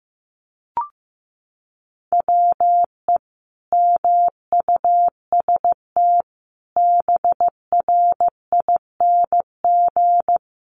Morse code, heard as a keyed tone, about 15 wpm, 700 Hz. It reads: WE MUST BRING
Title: Sentences from Top 1200 Words 15wpm